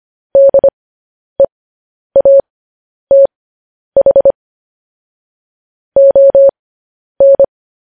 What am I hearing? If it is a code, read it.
Morse code, heard as a keyed tone, about 25 wpm, spaced as for 10 wpm, 550 Hz. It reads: DEATH ON
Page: News Headlines in Morse Code at 10 WPM